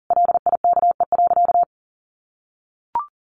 Morse code, heard as a keyed tone, about 40 wpm, 700 Hz. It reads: LIKE.
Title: Sentences from Top 500 Words 40wpm